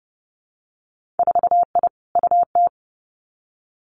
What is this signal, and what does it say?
Morse code, heard as a keyed tone, about 30 wpm, 700 Hz. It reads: 4I UT
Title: Winter Field Day Exchanges Encore 30wpm